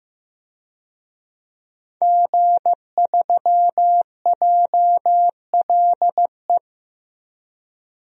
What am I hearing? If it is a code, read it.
Morse code, heard as a keyed tone, about 15 wpm, 700 Hz. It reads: G3JLE
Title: DXCC-100 Call Signs - 5 Characters - 15wpm